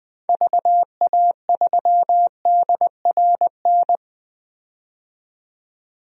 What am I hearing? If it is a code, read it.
Morse code, heard as a keyed tone, about 20 wpm, 700 Hz. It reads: VA3DRN
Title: DXCC-100 Call Signs - 6 Characters - 20wpm